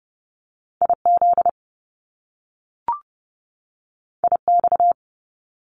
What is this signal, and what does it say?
Morse code, heard as a keyed tone, about 30 wpm, 700 Hz. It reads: IZ IX
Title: Top Two Letter Combinations in Words 30wpm